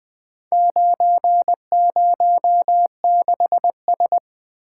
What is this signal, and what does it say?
Morse code, heard as a keyed tone, about 20 wpm, 700 Hz. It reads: 906S
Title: Four Letter-Number Combinations - 20wpm